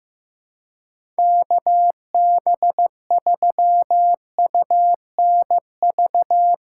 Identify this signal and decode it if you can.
Morse code, heard as a keyed tone, about 16 wpm, 700 Hz. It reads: KB3UNV
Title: USA Call Signs - 6 Characters - 2x Character Spacing - 15wpm